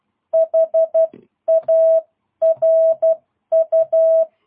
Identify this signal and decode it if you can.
Morse code, heard as a keyed tone, about 12 wpm, 660 Hz. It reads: HARU
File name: jh8cbh / 2022cw